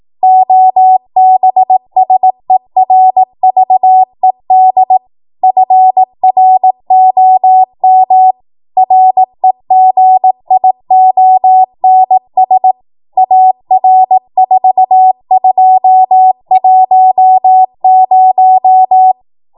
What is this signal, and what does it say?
Morse code, heard as a keyed tone, about 18 wpm, 750 Hz. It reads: OBSERVED FROM REGIONS AR4210
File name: Morse / 18WPM